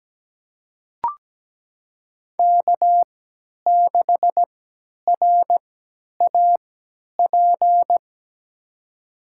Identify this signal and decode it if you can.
Morse code, heard as a keyed tone, about 17 wpm, spaced as for 10 wpm, 700 Hz. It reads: K6RAP
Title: USA Call Signs - 5 Characters - 3x Character Spacing - 17wpm